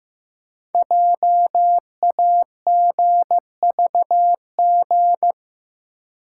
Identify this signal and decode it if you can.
Morse code, heard as a keyed tone, about 15 wpm, 700 Hz. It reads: JAGVG
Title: Code Groups - 5 Letters - 15wpm